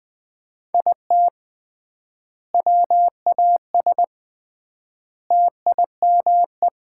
Morse code, heard as a keyed tone, about 20 wpm, 700 Hz. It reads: IT WAS TIME
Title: 3x Word-Spacing - Sets of 4 Words - 20wpm